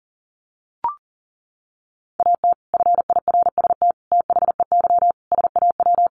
Morse code, heard as a keyed tone, about 40 wpm, 700 Hz. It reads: AT FIRST THEY SAW
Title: Sentences from Top 600 Words 40wpm